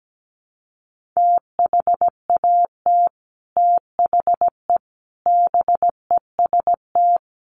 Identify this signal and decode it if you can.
Morse code, heard as a keyed tone, about 17 wpm, 700 Hz. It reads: THAT THE BEST
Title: Sets of 5 Words Encore2 17wpm